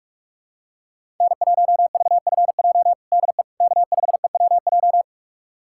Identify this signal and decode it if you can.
Morse code, heard as a keyed tone, about 46 wpm, 700 Hz. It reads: N1VFJ DE K5EWJ
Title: USA Callsign DE Callsign - 5 Characters - 45wpm